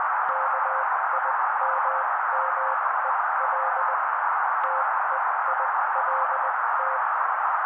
Morse code, heard as a keyed tone, about 20 wpm, spaced as for 16 wpm, 530 Hz. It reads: KIMMEL TEILT
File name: daily / 16wpm noise